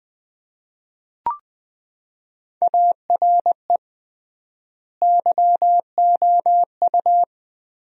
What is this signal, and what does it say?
Morse code, heard as a keyed tone, about 20 wpm, 700 Hz. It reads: ARE YOU
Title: 3x Word-Spacing - Sentences from Top 300 Words - 20wpm